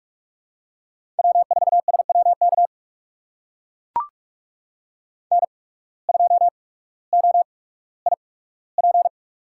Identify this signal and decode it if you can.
Morse code, heard as a keyed tone, about 44 wpm, spaced as for 19 wpm, 700 Hz. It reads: W4SWK N2OIP
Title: USA Call Signs - 5 Characters - 8x Character Spacing - 45wpm